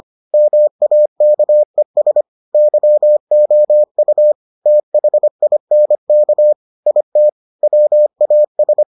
Morse code, heard as a keyed tone, about 25 wpm, 600 Hz. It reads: MAKES YOU THINK IT WAS